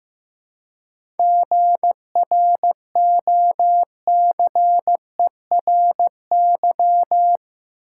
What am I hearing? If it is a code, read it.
Morse code, heard as a keyed tone, about 15 wpm, 700 Hz. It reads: GROCERY
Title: Seven Letter Words - 15wpm